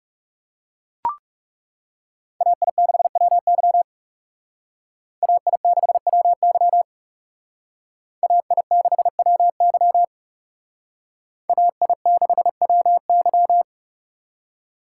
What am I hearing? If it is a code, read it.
Morse code, heard as a keyed tone, about 38 wpm, 700 Hz. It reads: AI6WY AI6WY AI6WY AI6WY